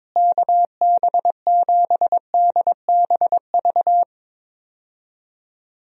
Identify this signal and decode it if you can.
Morse code, heard as a keyed tone, about 22 wpm, 700 Hz. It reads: KB7DBV
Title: USA Call Signs - 6 Characters - Repeated 2x - 22wpm